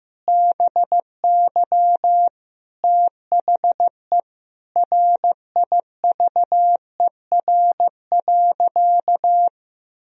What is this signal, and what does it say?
Morse code, heard as a keyed tone, about 15 wpm, 700 Hz. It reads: BY THE RIVER.
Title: Sentences from Top 1000 Words - Review Part 4 - 15wpm